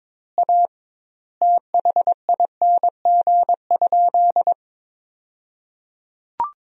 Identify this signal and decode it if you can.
Morse code, heard as a keyed tone, about 22 wpm, 700 Hz. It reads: A THING?
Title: 2x Word-Spacing - Sentences from Top 200 Words - 22wpm